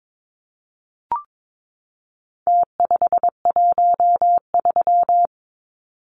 Morse code, heard as a keyed tone, about 22 wpm, 700 Hz. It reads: T513
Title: Numbers - Contest Style - 4-Digit 22wpm